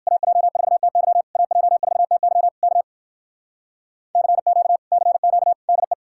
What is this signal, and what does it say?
Morse code, heard as a keyed tone, about 57 wpm, 700 Hz. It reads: AJ4TY AJ4TY K CQ CQ DE